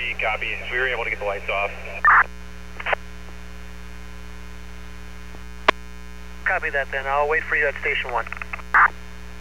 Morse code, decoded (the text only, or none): none